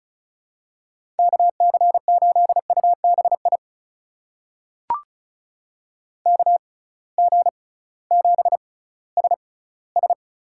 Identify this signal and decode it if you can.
Morse code, heard as a keyed tone, about 36 wpm, spaced as for 19 wpm, 700 Hz. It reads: KC8UBI KG7SS
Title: USA Call Signs - 6 Characters - 6x Character Spacing - 35wpm